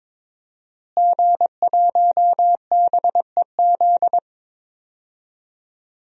Morse code, heard as a keyed tone, about 22 wpm, 700 Hz. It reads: G1BEZ
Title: DXCC-100 Call Signs - 5 Characters - Repeated 2x - 22wpm